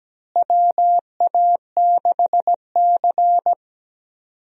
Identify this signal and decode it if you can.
Morse code, heard as a keyed tone, about 17 wpm, 700 Hz. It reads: WA6C
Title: USA Call Signs - 4 Characters - Repeated 3x - 17wpm